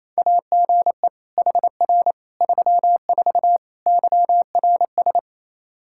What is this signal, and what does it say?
Morse code, heard as a keyed tone, about 28 wpm, 700 Hz. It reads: AGE HR 34 YRS
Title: Practice QSO - Exchange 3 - More Info - 28wpm